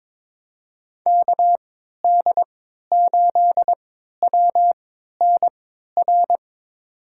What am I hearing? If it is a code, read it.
Morse code, heard as a keyed tone, about 22 wpm, spaced as for 12 wpm, 700 Hz. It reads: KD8WNR
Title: USA Call Signs - 6 Characters - 3x Character Spacing - 22wpm